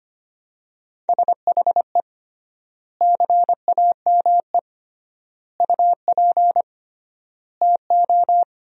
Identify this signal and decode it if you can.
Morse code, heard as a keyed tone, about 25 wpm, 700 Hz. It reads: SHE CAME UP TO
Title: 3x Word-Spacing - Sentences from Top 200 Words - 25wpm